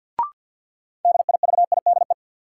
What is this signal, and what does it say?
Morse code, heard as a keyed tone, about 52 wpm, 700 Hz. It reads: DIVIDE
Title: Sentences from Top 1600 Words - New Word Drill - 50wpm